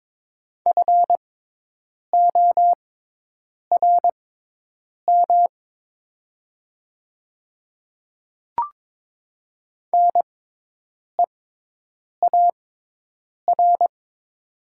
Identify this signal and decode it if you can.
Morse code, heard as a keyed tone, about 22 wpm, spaced as for 8 wpm, 700 Hz. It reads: FORM NEAR
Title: 6x Character Spacing - Top 200 Words - 22wpm